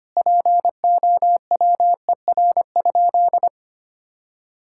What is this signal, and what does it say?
Morse code, heard as a keyed tone, about 25 wpm, 700 Hz. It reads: POWER?
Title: Sentences from Top 1000 Words - Review Part 5 - 25wpm